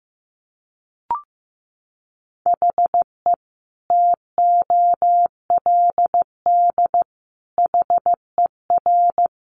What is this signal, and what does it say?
Morse code, heard as a keyed tone, about 15 wpm, 700 Hz. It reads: HE TOLD HER
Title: Sentences from Top 1500 Words 15wpm